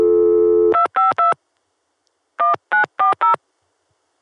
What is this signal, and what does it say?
DTMF, heard as telephone touch tones, tones of about 140 ms. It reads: 252 1940